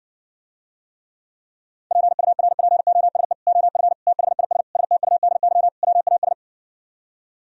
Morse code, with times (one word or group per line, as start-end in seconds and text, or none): 1.9-3.3 PURPOSE
3.5-3.9 OF
4.1-4.6 THIS
4.8-5.7 STUDY
5.8-6.3 WAS